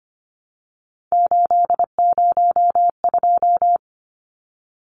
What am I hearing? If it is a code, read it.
Morse code, heard as a keyed tone, about 25 wpm, 700 Hz. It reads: 802
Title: Numbers - 3-Digit 25wpm